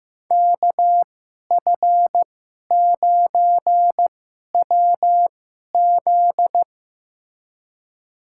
Morse code, heard as a keyed tone, about 15 wpm, spaced as for 11 wpm, 700 Hz. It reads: KF9WZ
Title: USA Call Signs - 2x Character Spacing - 15wpm